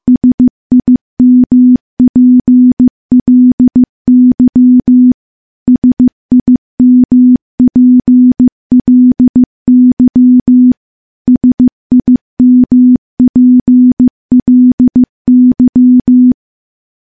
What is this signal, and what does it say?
Morse code, heard as a keyed tone, about 15 wpm, 250 Hz. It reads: SIMPLY SIMPLY SIMPLY
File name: morse-training-15WPM